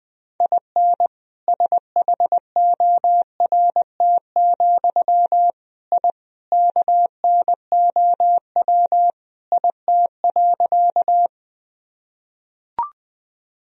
Morse code, heard as a keyed tone, about 21 wpm, 700 Hz. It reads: IN SHORT, I KNOW IT.
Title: Sentences from Top 1000 Words - Review Part 2 - 20wpm